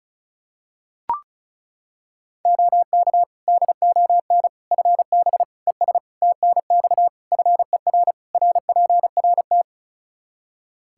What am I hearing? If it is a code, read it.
Morse code, heard as a keyed tone, about 34 wpm, 700 Hz. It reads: OK DON FB ES TNX FER RPRT
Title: Practice QSO - Exchange 2 - Info - 35wpm